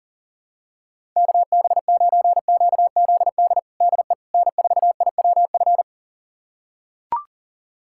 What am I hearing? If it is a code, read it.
Morse code, heard as a keyed tone, about 40 wpm, 700 Hz. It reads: KB9QZD DE N4IWF
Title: USA Callsign DE Callsign 40wpm